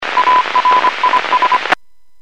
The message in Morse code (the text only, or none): ARIS